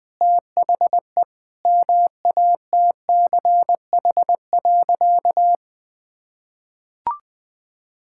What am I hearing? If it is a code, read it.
Morse code, heard as a keyed tone, about 20 wpm, 700 Hz. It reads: THE MATCH.
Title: Sentences with 5-Word Phrases - 20wpm